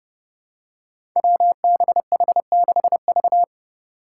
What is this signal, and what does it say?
Morse code, heard as a keyed tone, about 30 wpm, 700 Hz. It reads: WBH6V
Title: Five Challenging Letter-Number Combinations 30wpm